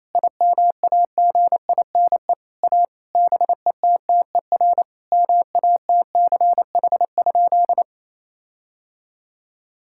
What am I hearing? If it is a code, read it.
Morse code, heard as a keyed tone, about 28 wpm, 700 Hz. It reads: IMAGINE A BETTER MATCH?
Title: Sentences from Top 1600 Words 28wpm